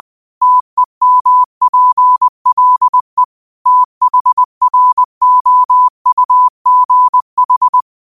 Morse code, heard as a keyed tone, about 20 wpm, 1000 Hz. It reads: TEMPLE THROUGH